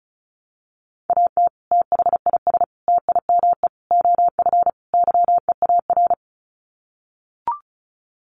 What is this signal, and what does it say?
Morse code, heard as a keyed tone, about 34 wpm, 700 Hz. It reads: AT THIS TIME OF YEAR
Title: Sets of 5 Words Encore2 35wpm